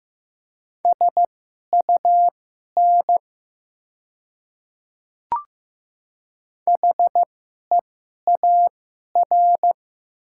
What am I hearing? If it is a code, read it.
Morse code, heard as a keyed tone, about 15 wpm, spaced as for 11 wpm, 700 Hz. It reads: SUN HEAR
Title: 2x Character Spacing - Top 300 Words - 15wpm